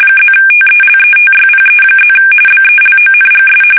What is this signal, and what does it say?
RTTY, about 45.45 baud. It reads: +TELETYPE+ BELONGS